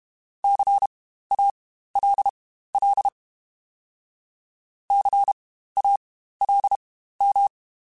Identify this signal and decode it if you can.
Morse code, heard as a keyed tone, about 32 wpm, spaced as for 15 wpm, 800 Hz. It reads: CALL CALM
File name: Spaced Alphas / Spaced Alphas mp3s / Cs sp @30